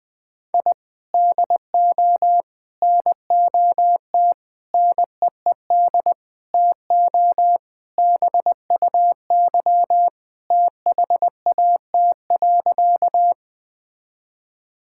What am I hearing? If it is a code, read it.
Morse code, heard as a keyed tone, about 20 wpm, 700 Hz. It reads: I DO NOT NEED TO BUY THAT.
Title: Sentences with 5-Word Phrases - 20wpm